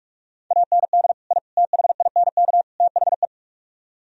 Morse code, heard as a keyed tone, about 46 wpm, 690 Hz. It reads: AND I THINK THE